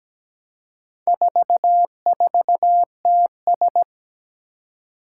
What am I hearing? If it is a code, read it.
Morse code, heard as a keyed tone, about 17 wpm, 700 Hz. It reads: 44TS
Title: Four Challenging Letter-Number Combinations 17wpm